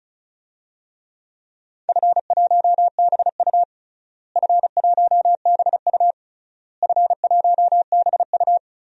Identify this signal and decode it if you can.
Morse code, heard as a keyed tone, about 36 wpm, 700 Hz. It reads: F1BU F1BU F1BU
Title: DXCC-100 Call Signs - 4 Characters - Repeated 3x - 35wpm